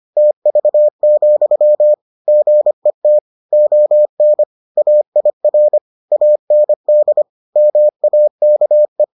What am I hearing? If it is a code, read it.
Morse code, heard as a keyed tone, about 25 wpm, 600 Hz. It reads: TV, GET ON AIR AND MAKE